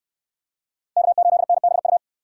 Morse code, heard as a keyed tone, about 64 wpm, 700 Hz. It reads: K8NBU